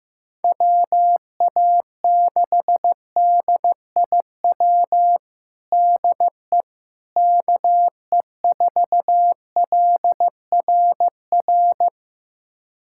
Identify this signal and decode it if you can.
Morse code, heard as a keyed tone, about 15 wpm, 700 Hz. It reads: WA6DIW DE KE4LRR
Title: USA Callsign DE Callsign 15wpm